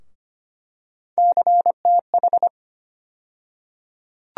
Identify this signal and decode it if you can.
Morse code, heard as a keyed tone, about 25 wpm, 700 Hz. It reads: CTH
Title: USA Call Sign Postfixes - 3 Letters 25wpm